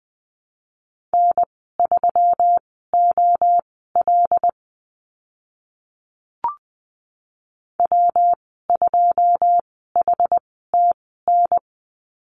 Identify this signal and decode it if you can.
Morse code, heard as a keyed tone, about 20 wpm, spaced as for 14 wpm, 700 Hz. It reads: N3OL W2HTN
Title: 122 Lesson 20g - Number 3 - Call Signs 20wpm